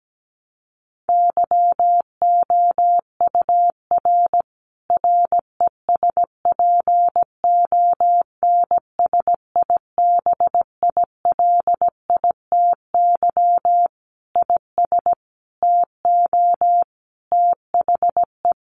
Morse code, heard as a keyed tone, about 17 wpm, 700 Hz. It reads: YOUR RESPONSIBILITY IS TO THE
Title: Sentences from Top 1200 Words 17wpm